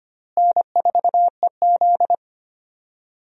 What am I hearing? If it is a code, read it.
Morse code, heard as a keyed tone, about 25 wpm, 700 Hz. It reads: N4EZ
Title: USA Call Signs - 4 Characters - Repeated 3x - 25wpm